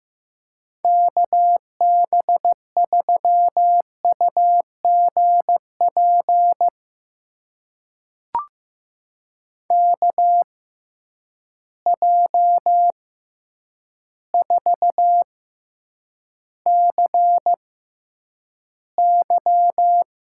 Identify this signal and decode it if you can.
Morse code, heard as a keyed tone, about 15 wpm, spaced as for 8 wpm, 700 Hz. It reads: KB3UGP KJ4CY
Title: USA Call Signs - 6 Characters - 6x Character Spacing - 15wpm